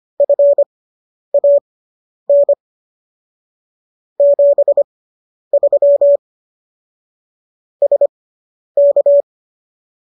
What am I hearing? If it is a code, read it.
Morse code, heard as a keyed tone, about 25 wpm, spaced as for 10 wpm, 570 Hz. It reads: FAN 73 SK